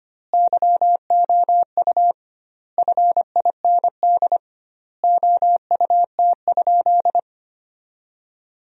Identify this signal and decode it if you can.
Morse code, heard as a keyed tone, about 25 wpm, 700 Hz. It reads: YOU FIND OUT?
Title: 2x Word-Spacing - Sentences from Top 100 Words - 25wpm